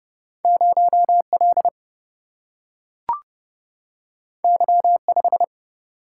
Morse code, heard as a keyed tone, about 30 wpm, 700 Hz. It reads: 0 L Y 5
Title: Single-Letter Single-Number 30wpm